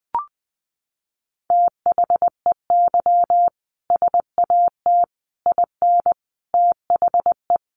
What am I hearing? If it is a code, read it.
Morse code, heard as a keyed tone, about 20 wpm, 700 Hz. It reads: THEY SAT IN THE
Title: A Sound Of Thunder 20wpm